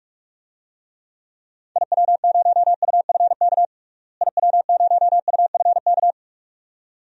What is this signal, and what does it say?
Morse code, heard as a keyed tone, about 43 wpm, 700 Hz. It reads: IW0UFK IW0UFK
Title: DXCC-100 Call Signs - 6 Characters - Repeated 2x - 45wpm